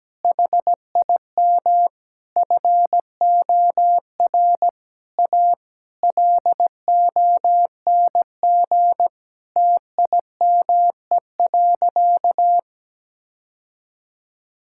Morse code, heard as a keyed tone, about 17 wpm, 700 Hz. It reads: HIM FOR A LONG TIME.
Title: Sentences from Top 1600 Words 17wpm